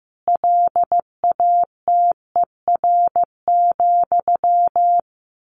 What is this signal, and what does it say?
Morse code, heard as a keyed tone, about 15 wpm, 700 Hz. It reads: LATER,
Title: A Sound Of Thunder 15wpm